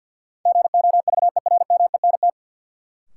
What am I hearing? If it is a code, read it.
Morse code, heard as a keyed tone, about 48 wpm, 700 Hz. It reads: GOVERMENT
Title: Sentences from Top 1300 Words - New Word Drill Alt - 50wpm